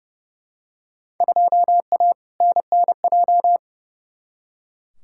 Morse code, heard as a keyed tone, about 32 wpm, 700 Hz. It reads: 2A NNJ
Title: Field Day Exchanges - 30wpm